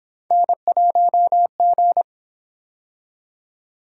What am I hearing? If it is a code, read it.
Morse code, heard as a keyed tone, about 28 wpm, 700 Hz. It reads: N1G